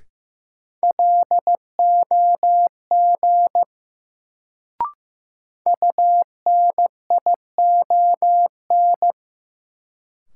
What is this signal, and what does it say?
Morse code, heard as a keyed tone, about 15 wpm, 700 Hz. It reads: LOG UNION
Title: Sentences from Top 1400 Words - New Word Drill Alt - 15wpm